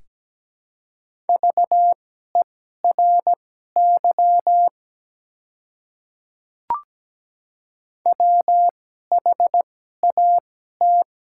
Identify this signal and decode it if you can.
Morse code, heard as a keyed tone, about 17 wpm, spaced as for 12 wpm, 700 Hz. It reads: VERY WHAT
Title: 2x Character Spacing - Top 200 Words - 17wpm